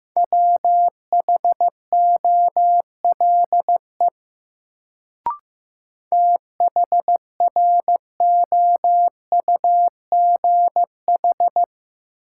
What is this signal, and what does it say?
Morse code, heard as a keyed tone, about 15 wpm, 700 Hz. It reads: WHOLE THROUGH
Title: Sets of 2 Words - New Word Drill - 15wpm